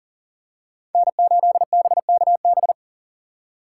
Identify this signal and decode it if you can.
Morse code, heard as a keyed tone, about 40 wpm, 700 Hz. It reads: N8BKB
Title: USA Call Signs - 3x Character Spacing - 40wpm